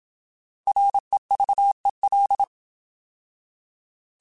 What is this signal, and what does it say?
Morse code, heard as a keyed tone, about 27 wpm, 800 Hz. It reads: REVEL
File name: Alphabetical / Alphabetical mp3s / Rs @25